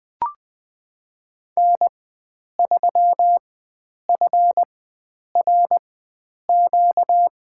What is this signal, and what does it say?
Morse code, heard as a keyed tone, about 20 wpm, spaced as for 9 wpm, 700 Hz. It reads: N3FRQ
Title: USA Call Signs - 4x Character Spacing - 20wpm